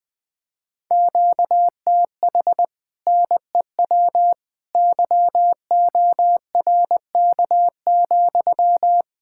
Morse code, heard as a keyed tone, about 21 wpm, 700 Hz. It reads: QTH NEW YORK,